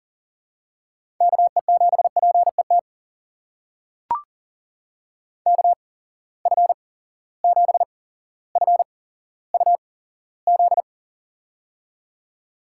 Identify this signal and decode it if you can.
Morse code, heard as a keyed tone, about 40 wpm, spaced as for 17 wpm, 700 Hz. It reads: KE7PET KF7FUZ
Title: USA Call Signs - 6 Characters - 8x Character Spacing - 40wpm